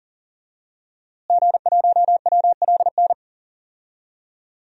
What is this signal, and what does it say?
Morse code, heard as a keyed tone, about 40 wpm, 700 Hz. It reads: G1WLN
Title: DXCC-100 Call Signs - 5 Characters - Repeated 3x - 40wpm